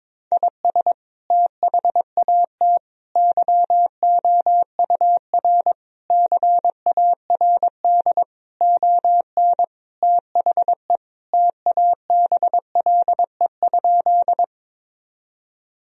Morse code, heard as a keyed tone, about 21 wpm, 700 Hz. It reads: IS THAT YOUR CARD ON THE TABLE?
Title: Sentences from Top 1000 Words - Review Part 4 - 22wpm